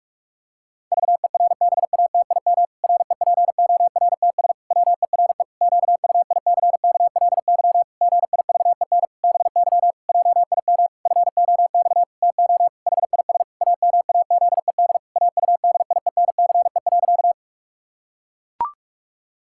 Morse code, heard as a keyed tone, about 41 wpm, 700 Hz. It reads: VERBATIM REPORTS WERE QUICKLY GIVEN BY JIM FOX TO HIS AMAZED AUDIENCE.
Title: Pangram Challenge v2 - 45wpm